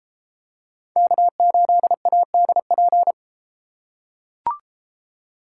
Morse code, heard as a keyed tone, about 32 wpm, 700 Hz. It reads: K8ADP